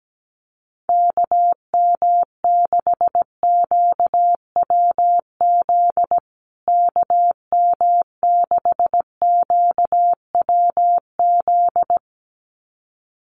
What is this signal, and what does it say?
Morse code, heard as a keyed tone, about 17 wpm, 700 Hz. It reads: KM6QWZ KM6QWZ
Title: USA Call Signs Repeated 2x 17wpm